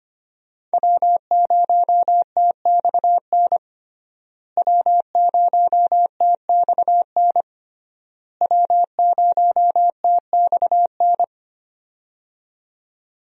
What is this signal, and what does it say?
Morse code, heard as a keyed tone, about 25 wpm, 700 Hz. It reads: W0TXN W0TXN W0TXN
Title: USA Call Signs - 5 Characters - Repeated 3x - 25wpm